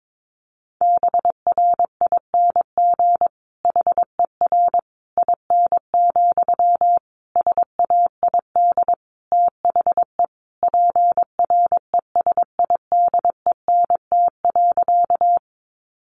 Morse code, heard as a keyed tone, about 22 wpm, 700 Hz. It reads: BRING HER IN, SAID THE PRESIDENT.
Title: Sentences from Top 1000 Words 22wpm